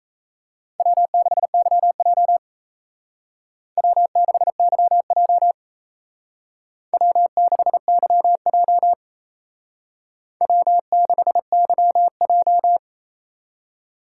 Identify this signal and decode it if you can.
Morse code, heard as a keyed tone, about 35 wpm, 700 Hz. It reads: W6YJ W6YJ W6YJ W6YJ